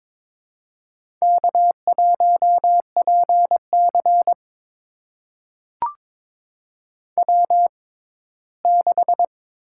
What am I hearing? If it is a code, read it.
Morse code, heard as a keyed tone, about 22 wpm, spaced as for 15 wpm, 700 Hz. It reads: K1PC W6
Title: USA Call Signs - 6x Character Spacing - 22wpm